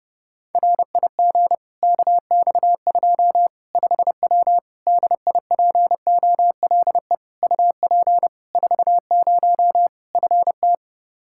Morse code, heard as a keyed tone, about 30 wpm, 700 Hz. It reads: RIG KX2 5W DIPOLE UP 40 FT
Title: Practice QSO - Exchange 2 - Info - 30wpm